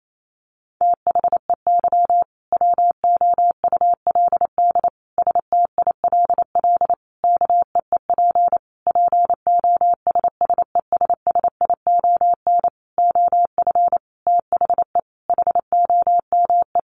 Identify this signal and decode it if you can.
Morse code, heard as a keyed tone, about 28 wpm, 700 Hz. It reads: THEY WOULD STILL KEEP POSSESSION OF THE HOME
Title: Sentences from Top 1400 Words 28wpm